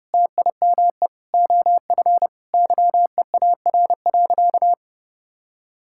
Morse code, heard as a keyed tone, about 30 wpm, 700 Hz. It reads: TIME OF YEAR.
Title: Sentences from Top 1000 Words - Review Part 6 - 30wpm